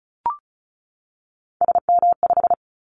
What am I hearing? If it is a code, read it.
Morse code, heard as a keyed tone, about 36 wpm, 700 Hz. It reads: SM5
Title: Callsign Prefixes for Top 25 DXCC - 35wpm